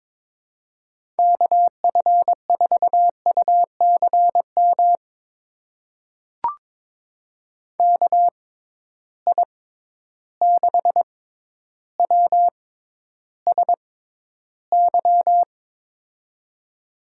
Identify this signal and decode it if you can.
Morse code, heard as a keyed tone, about 22 wpm, spaced as for 11 wpm, 700 Hz. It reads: KF4UCM KI6WSY
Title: USA Call Signs - 6 Characters - 6x Character Spacing - 22wpm